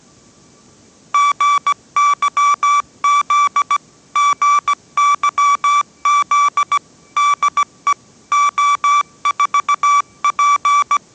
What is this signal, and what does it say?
Morse code, heard as a keyed tone, about 17 wpm, 1160 Hz. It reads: GYZ GYZ DE O4P